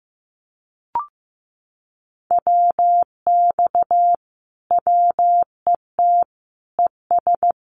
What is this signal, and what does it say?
Morse code, heard as a keyed tone, about 15 wpm, 700 Hz. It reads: WX WET ES
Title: Common QSO Phrases Encore 15wpm